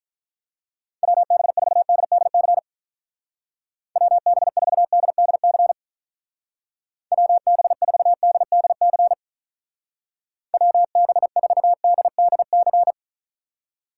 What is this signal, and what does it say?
Morse code, heard as a keyed tone, about 46 wpm, 700 Hz. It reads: WB4DDC WB4DDC WB4DDC WB4DDC